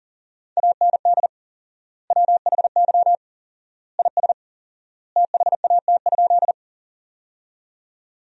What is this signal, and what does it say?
Morse code, heard as a keyed tone, about 41 wpm, 700 Hz. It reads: AND WHY IS THAT?
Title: 4x Word-Spacing - Sentences from Top 200 Words - 40wpm